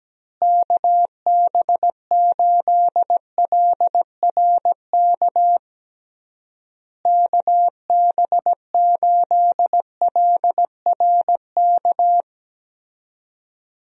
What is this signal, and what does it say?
Morse code, heard as a keyed tone, about 17 wpm, 700 Hz. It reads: KB8LRK KB8LRK
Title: USA Call Signs - 6 Characters - Repeated 3x - 17wpm